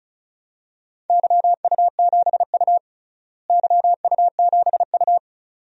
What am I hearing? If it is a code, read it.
Morse code, heard as a keyed tone, about 36 wpm, 700 Hz. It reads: YU7U YU7U
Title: DXCC-100 Call Signs - 4 Characters - Repeated 2x - 35wpm